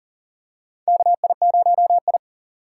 Morse code, heard as a keyed tone, about 40 wpm, 700 Hz. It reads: KI0I
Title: K1USN SST QSO Practice - 40wpm